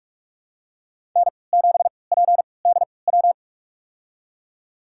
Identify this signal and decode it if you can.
Morse code, heard as a keyed tone, about 46 wpm, spaced as for 26 wpm, 700 Hz. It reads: N7PDW
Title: USA Call Signs - 5 Characters - 3x Character Spacing - 45wpm